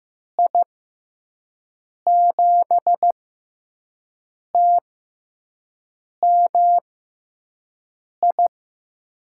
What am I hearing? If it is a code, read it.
Morse code, heard as a keyed tone, about 15 wpm, spaced as for 5 wpm, 700 Hz. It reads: I7TMI
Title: USA Call Signs - 6 Characters - 6x Character Spacing - 15wpm